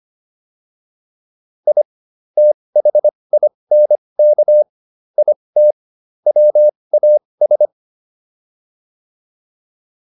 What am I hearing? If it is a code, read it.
Morse code, heard as a keyed tone, about 25 wpm, spaced as for 20 wpm, 600 Hz. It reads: I THINK IT WAS
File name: WD304 20